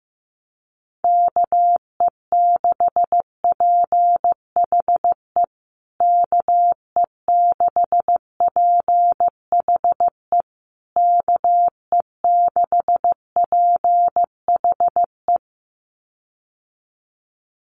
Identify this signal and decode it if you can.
Morse code, heard as a keyed tone, about 15 wpm, 700 Hz. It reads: KE6PHE KE6PHE KE6PHE
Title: USA Call Signs Repeated 3x 15wpm